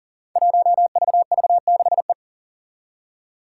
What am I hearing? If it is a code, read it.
Morse code, heard as a keyed tone, about 42 wpm, 690 Hz. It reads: 1VV6E
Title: Five Challenging Letter-Number Combinations 40wpm